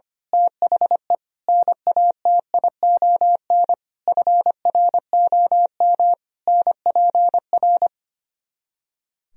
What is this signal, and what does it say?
Morse code, heard as a keyed tone, about 24 wpm, 700 Hz. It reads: THE NATION FROM NPR
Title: Speed-Racing - Sets of 5 Words 25wpm